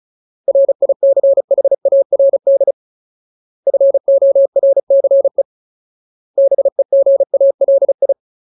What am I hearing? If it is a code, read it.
Morse code, heard as a keyed tone, about 34 wpm, 550 Hz. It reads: RICHARD FORCE BEGALI
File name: Ultimate Word List / 35WPM Ultimate